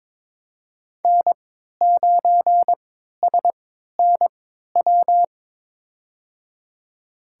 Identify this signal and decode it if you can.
Morse code, heard as a keyed tone, about 21 wpm, spaced as for 12 wpm, 700 Hz. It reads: N9SNW
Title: USA Call Signs - 5 Characters - 3x Character Spacing - 22wpm